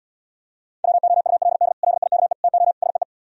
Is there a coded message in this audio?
Yes: Morse code from 0.8 to 3.0 s, heard as a keyed tone, about 77 wpm, 700 Hz. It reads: WOULD LIKE TO SEE